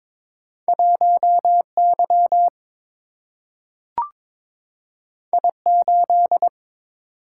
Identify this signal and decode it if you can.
Morse code, heard as a keyed tone, about 22 wpm, 700 Hz. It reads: 1 Y I 8
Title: Single-Letter Single-Number 22wpm